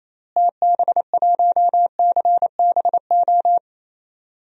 Morse code, heard as a keyed tone, about 28 wpm, 700 Hz. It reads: TB1CBO
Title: DXCC-100 Call Signs - 6 Characters - Repeated 3x - 28wpm